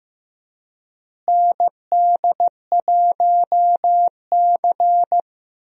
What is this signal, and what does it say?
Morse code, heard as a keyed tone, about 15 wpm, 700 Hz. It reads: ND1C
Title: USA Call Signs - 4 Characters - Repeated 3x - 15wpm